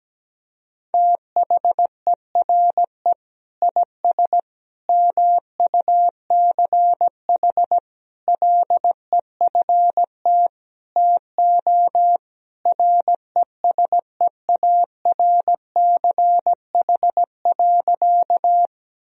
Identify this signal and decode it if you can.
Morse code, heard as a keyed tone, about 17 wpm, 700 Hz. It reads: THERE IS MUCH LEFT TO RESEARCH.
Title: Sentences from Top 1000 Words - Review Part 4 - 17wpm